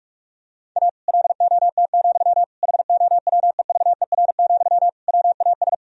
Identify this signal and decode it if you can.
Morse code, heard as a keyed tone, about 41 wpm, 700 Hz. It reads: A LOT, HOWEVER, WAS
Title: Sentences from Top 1000 Words - Review Part 3 - 45wpm